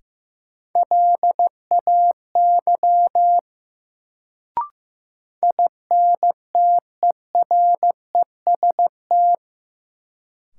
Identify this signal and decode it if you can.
Morse code, heard as a keyed tone, about 15 wpm, 700 Hz. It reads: LAY INTEREST
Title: Sentences from Top 400 Words - New Word Drill - 15wpm